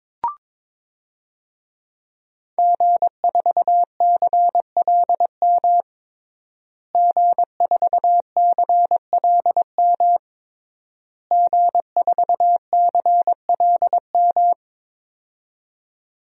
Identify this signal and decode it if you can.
Morse code, heard as a keyed tone, about 22 wpm, 700 Hz. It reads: G4CLM G4CLM G4CLM
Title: DXCC-100 Call Signs - 5 Characters - Repeated 3x - 22wpm